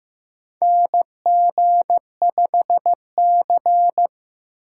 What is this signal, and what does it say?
Morse code, heard as a keyed tone, about 15 wpm, 700 Hz. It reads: NG5C
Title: USA Call Signs - 4 Characters - 4x Character Spacing - 15wpm